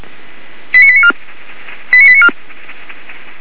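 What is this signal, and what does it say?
Five-tone selective call (ZVEI), heard as quick runs of five tones, tones of about 70 ms. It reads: 89894 89894
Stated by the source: ZVEI five-tone selective call received with an SDR, Feuerwehr Orlamünde fire brigade.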